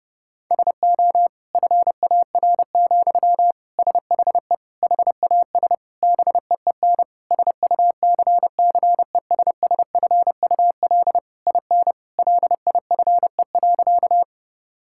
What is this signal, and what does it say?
Morse code, heard as a keyed tone, about 30 wpm, 700 Hz. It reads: SO FAR, SHE HAS BEEN SUCCESSFUL IN LIFE.
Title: Sentences from Top 1000 Words - Review Part 6 - 30wpm